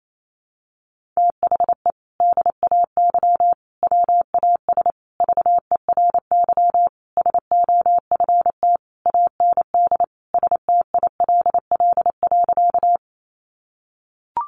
Morse code, heard as a keyed tone, about 28 wpm, 700 Hz. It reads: THE DAY WAS VERY SOFT AND STILL.
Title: Sentences from Top 1500 Words 28wpm